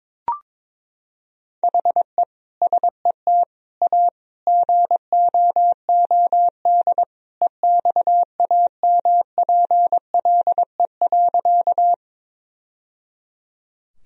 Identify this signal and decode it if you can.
Morse code, heard as a keyed tone, about 22 wpm, 700 Hz. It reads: HE SET A GOOD EXAMPLE.
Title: Sentences from Top 1000 Words - Review Part 1 - 22wpm